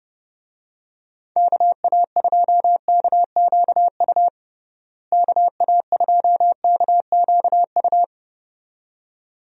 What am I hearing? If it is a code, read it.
Morse code, heard as a keyed tone, about 32 wpm, 700 Hz. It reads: KA2KQU KA2KQU
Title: USA Call Signs - 6 Characters - Repeated 2x - 30wpm